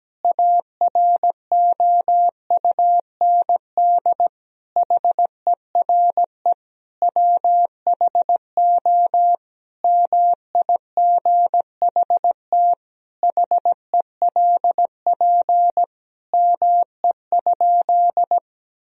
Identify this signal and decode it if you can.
Morse code, heard as a keyed tone, about 17 wpm, 700 Hz. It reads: AROUND HERE WHO MIGHT HELP ME?
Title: Sentences from Top 1000 Words - Review Part 3 - 17wpm